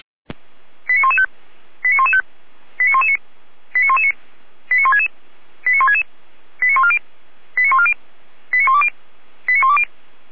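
Five-tone selective call (ZVEI), heard as quick runs of five tones, tones of about 70 ms. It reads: 89115 89115 89119 89119 89166 89166 89140 89140 89120 89120
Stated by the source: ZVEI five-tone selective call 89099 received by SDR, from the Feuerwehr Orlamünde audio server.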